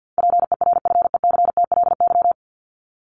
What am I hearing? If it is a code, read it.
Morse code, heard as a keyed tone, about 50 wpm, 700 Hz. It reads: PERFECTLY
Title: LY Suffix - Top 100 Words with Base Words 50wpm